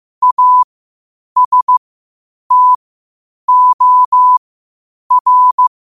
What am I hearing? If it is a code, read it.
Morse code, heard as a keyed tone, about 15 wpm, spaced as for 8 wpm, 1000 Hz. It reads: ASTOR
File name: qotdfiles / 8wpm